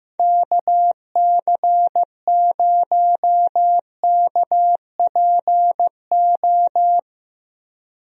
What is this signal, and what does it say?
Morse code, heard as a keyed tone, about 16 wpm, 700 Hz. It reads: KC0KPO
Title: USA Call Signs - 6 Characters - Repeated 3x - 15wpm